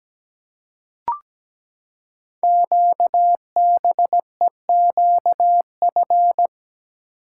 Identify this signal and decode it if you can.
Morse code, heard as a keyed tone, about 17 wpm, 700 Hz. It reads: QBEQF
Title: Code Groups - 5 Letters - 17wpm